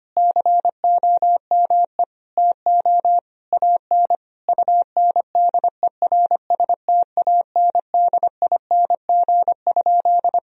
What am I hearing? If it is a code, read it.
Morse code, heard as a keyed tone, about 25 wpm, 700 Hz. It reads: COME TO AN UNDERSTANDING?
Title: Sentences from Top 1000 Words - Review Part 3 - 25wpm